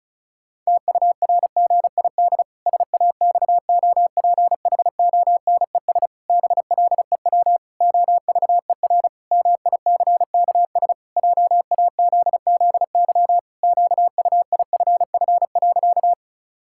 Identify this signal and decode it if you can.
Morse code, heard as a keyed tone, about 36 wpm, 700 Hz. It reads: TURGID SAXOPHONES BLEW OVER MICKS JAZZY QUIFF.
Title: Pangram Challenge v2 - 35wpm